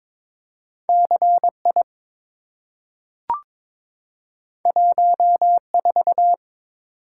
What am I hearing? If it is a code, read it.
Morse code, heard as a keyed tone, about 22 wpm, 700 Hz. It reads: CI 14
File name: practice / 22wpm / Two Letter-Number Combinations - 22wpm